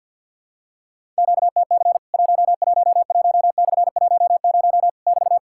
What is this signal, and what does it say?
Morse code, heard as a keyed tone, about 53 wpm, 700 Hz. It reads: QTC 111/10 <BT>
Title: Worked All Europe DX Contest - QTC Exchange with Repeats - 50wpm